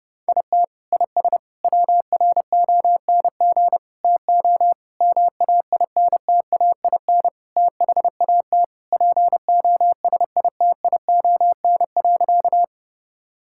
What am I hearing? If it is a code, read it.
Morse code, heard as a keyed tone, about 30 wpm, 700 Hz. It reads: IT IS WRONG TO MAINTAIN THAT POSITION.
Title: Sentences from Top 1000 Words - Review Part 5 - 30wpm